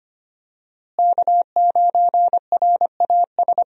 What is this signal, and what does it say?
Morse code, heard as a keyed tone, about 25 wpm, 700 Hz. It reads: K9RAS
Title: USA Call Signs - 5 Characters - Repeated 2x - 25wpm